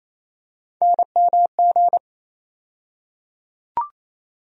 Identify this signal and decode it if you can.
Morse code, heard as a keyed tone, about 28 wpm, 700 Hz. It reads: NMG
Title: USA Call Sign Postfixes - 3 Letters 28wpm